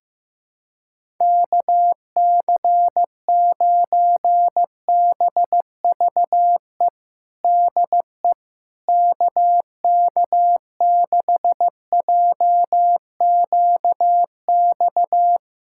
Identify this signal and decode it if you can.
Morse code, heard as a keyed tone, about 15 wpm, 700 Hz. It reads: KC9BVE DE KK6JQX
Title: USA Callsign DE Callsign - 6 Characters - 15wpm